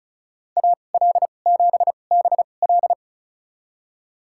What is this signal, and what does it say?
Morse code, heard as a keyed tone, about 34 wpm, spaced as for 25 wpm, 700 Hz. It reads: AL7BL
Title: USA Call Signs - 5 Characters - 2x Character Spacing - 35wpm